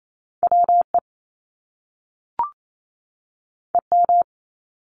Morse code, heard as a keyed tone, about 28 wpm, 700 Hz. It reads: WE EM
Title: Top Two Letter Combinations in Words 28wpm